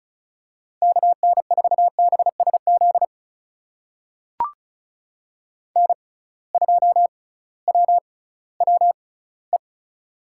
Morse code, heard as a keyed tone, about 36 wpm, spaced as for 19 wpm, 700 Hz. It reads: KN4BSZ N2WWE
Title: USA Call Signs - 6x Character Spacing - 35wpm